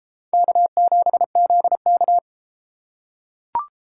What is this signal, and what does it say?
Morse code, heard as a keyed tone, about 33 wpm, 700 Hz. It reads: K7ZK